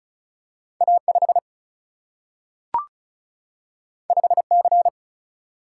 Morse code, heard as a keyed tone, about 36 wpm, 700 Hz. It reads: A5 5C
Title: Single-Letter Single-Number 35wpm